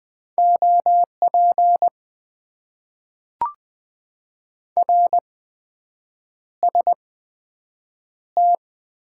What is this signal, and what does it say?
Morse code, heard as a keyed tone, about 20 wpm, spaced as for 7 wpm, 700 Hz. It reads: OP RST
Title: Top 90 QSO Elements - 8x Character Spacing - 20wpm